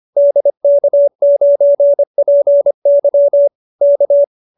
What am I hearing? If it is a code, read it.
Morse code, heard as a keyed tone, about 25 wpm, 590 Hz. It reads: DK9PY K
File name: QSO318 25